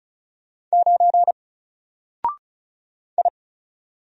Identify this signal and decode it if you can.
Morse code, heard as a keyed tone, about 34 wpm, 700 Hz. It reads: 9 I
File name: Single-Letter-Number-35wpm